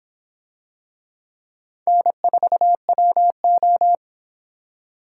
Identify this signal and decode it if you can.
Morse code, heard as a keyed tone, about 24 wpm, 700 Hz. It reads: N4WO